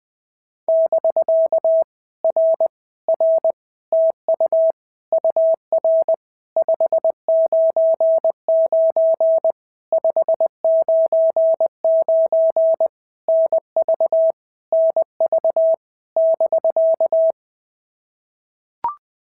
Morse code, heard as a keyed tone, about 20 wpm, 650 Hz. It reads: <BK> R R TU UR 599 599 NV NV <BK>
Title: POTA QSO Practice - 20wpm